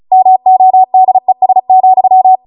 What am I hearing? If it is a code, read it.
Morse code, heard as a keyed tone, about 36 wpm, 750 Hz. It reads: MODES,